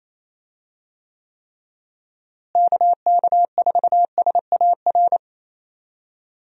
Morse code, heard as a keyed tone, about 28 wpm, 700 Hz. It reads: KK4SAR
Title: USA Call Signs - 6 Characters - Repeated 3x - 28wpm